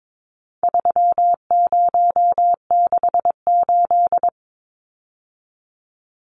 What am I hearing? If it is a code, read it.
Morse code, heard as a keyed tone, about 22 wpm, 700 Hz. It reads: 3068
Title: Numbers - 4-Digit 22wpm